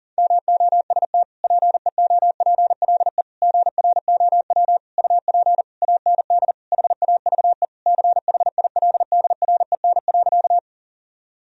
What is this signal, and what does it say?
Morse code, heard as a keyed tone, about 40 wpm, 700 Hz. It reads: MOST PEOPLE GROW UP AND HAVE CHILDREN.
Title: Sentences from Top 1000 Words - Review Part 1 - 40wpm